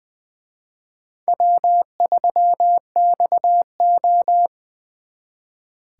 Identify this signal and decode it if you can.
Morse code, heard as a keyed tone, about 23 wpm, 700 Hz. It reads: W3XO
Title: Speed Racing - USA 1x2 Call Signs 20wpm